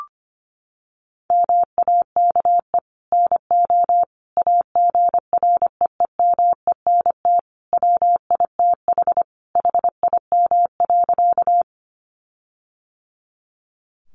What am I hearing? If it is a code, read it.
Morse code, heard as a keyed tone, about 25 wpm, 700 Hz. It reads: MAKE NO AGREEMENT WITH HIM.
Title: Sentences from Top 1500 Words 25wpm